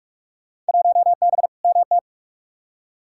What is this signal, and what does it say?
Morse code, heard as a keyed tone, about 45 wpm, 700 Hz. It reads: 1B MT
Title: Field Day Exchanges - 45wpm